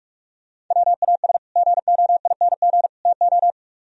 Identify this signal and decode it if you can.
Morse code, heard as a keyed tone, about 46 wpm, 700 Hz. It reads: WAS GOING TO